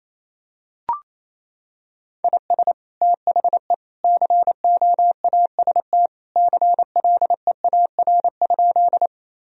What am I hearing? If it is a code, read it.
Morse code, heard as a keyed tone, about 28 wpm, 700 Hz. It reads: IS THE COAST CLEAR?
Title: Sentences from Top 1000 Words - Review Part 4 - 28wpm